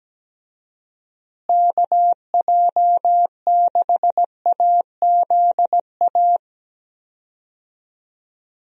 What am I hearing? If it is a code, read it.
Morse code, heard as a keyed tone, about 17 wpm, 700 Hz. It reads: KJ6AZA
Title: USA Call Signs - 6 Characters - Repeated 3x - 17wpm